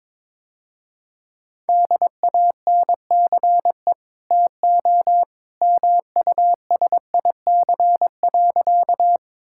Morse code, heard as a keyed tone, about 22 wpm, 700 Hz. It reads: DANCE TO MUSIC.
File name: practice / 22wpm / Easy Sentences with Hints - 22wpm